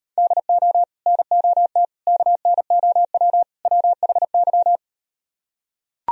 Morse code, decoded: DO NOT KNOW WHY

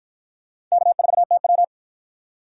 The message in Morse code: K4TW

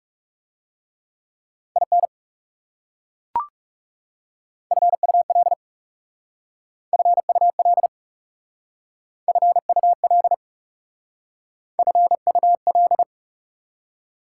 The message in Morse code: IN FUL FUL FUL FUL

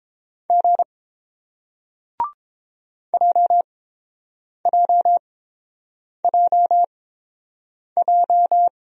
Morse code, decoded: G JJJJ

